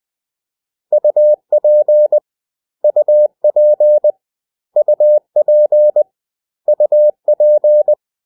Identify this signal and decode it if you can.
Morse code, heard as a keyed tone, about 20 wpm, 600 Hz. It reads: UP UP UP UP
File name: cw / 100 Common English Words